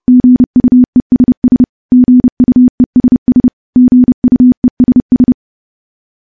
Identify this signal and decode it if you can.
Morse code, heard as a keyed tone, about 28 wpm, 250 Hz. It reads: GUESS GUESS GUESS
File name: morse / morse-training-30WPM